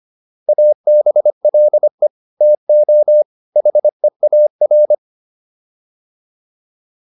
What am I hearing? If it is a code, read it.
Morse code, heard as a keyed tone, about 25 wpm, 600 Hz. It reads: ABLE TO HEAR